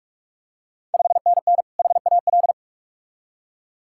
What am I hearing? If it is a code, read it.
Morse code, heard as a keyed tone, about 44 wpm, 690 Hz. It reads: 5NN HAL
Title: Ontario QSO Party Exchange 45wpm